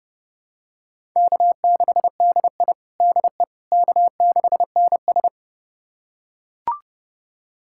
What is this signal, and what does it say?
Morse code, heard as a keyed tone, about 31 wpm, 700 Hz. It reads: K6DI DE K6NS
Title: USA Callsign DE Callsign - 4 Characters - 30wpm